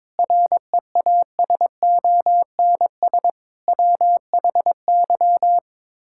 Morse code, heard as a keyed tone, about 22 wpm, 700 Hz. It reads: REASONS WHY
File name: practice / 22wpm / Sets of 7 Words - 22wpm